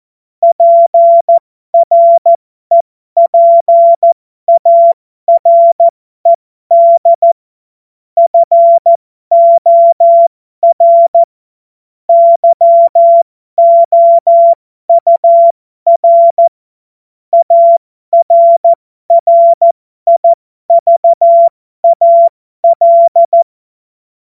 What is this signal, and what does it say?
Morse code, heard as a keyed tone, about 14 wpm, spaced as for 12 wpm, 680 Hz. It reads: PREPARED FOR YOUR ARRIVAL
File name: Morsekurs / Witze